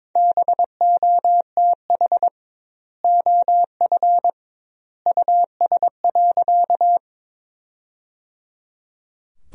Morse code, heard as a keyed tone, about 22 wpm, 700 Hz. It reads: BOTH OF US.
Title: 2x Word-Spacing - Sentences from Top 300 Words - 22wpm